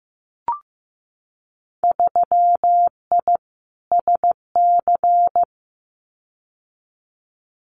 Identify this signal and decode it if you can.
Morse code, heard as a keyed tone, about 15 wpm, 700 Hz. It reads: 3I SC
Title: Winter Field Day Exchanges 15wpm